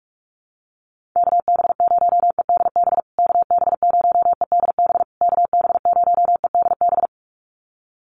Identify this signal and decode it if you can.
Morse code, heard as a keyed tone, about 44 wpm, 700 Hz. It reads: KB0EDB KB0EDB KB0EDB
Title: USA Call Signs Repeated 3x 45wpm